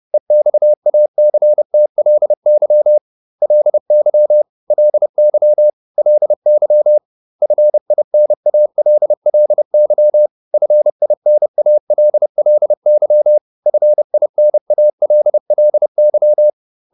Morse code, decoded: EXACTLY LY LY LY FINALLY FINALLY FINALLY

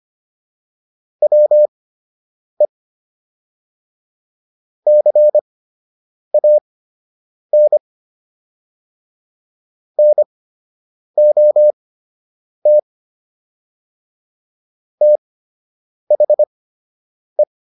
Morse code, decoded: WE CAN NOT THE

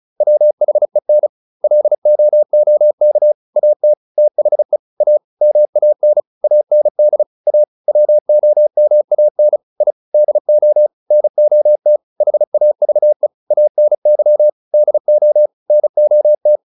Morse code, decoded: WHEN LOOK AT THE A MAN AND A WOMAN I DO NOT HAVE ANY DO NOT

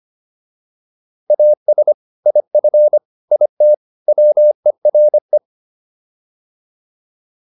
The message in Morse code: AS IF IT WERE

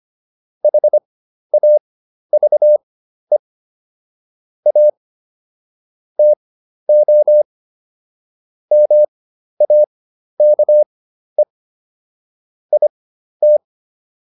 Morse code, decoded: HAVE A TO MAKE IT

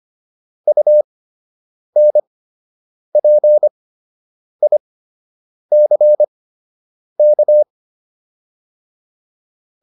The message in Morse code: UNPICK

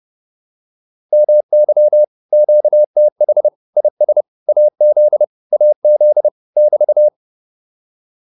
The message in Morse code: MY QTH IS AZ AZ <BT>

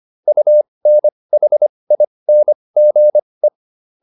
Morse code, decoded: UNHINGE

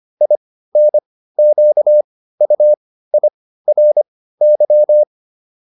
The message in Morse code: INQUIRY

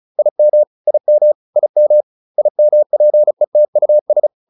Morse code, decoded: IM IM IM IMPETUS

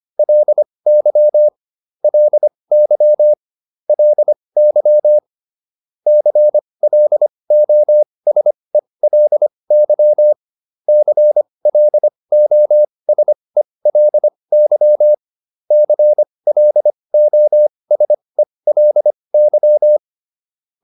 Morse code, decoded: LY LY LY CLOSELY CLOSELY CLOSELY

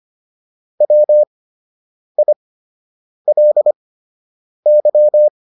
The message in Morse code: WILY

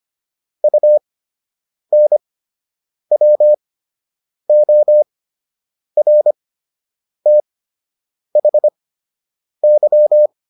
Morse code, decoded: UNWORTHY

